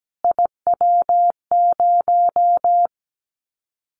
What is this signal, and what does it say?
Morse code, heard as a keyed tone, about 17 wpm, 700 Hz. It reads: IW0
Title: Callsign Prefixes for Top 25 DXCC - Repeated 3x - 17wpm